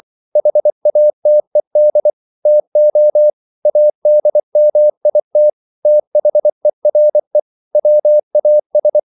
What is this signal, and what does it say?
Morse code, heard as a keyed tone, about 24 wpm, 600 Hz. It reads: HATED TO ADMIT THERE WAS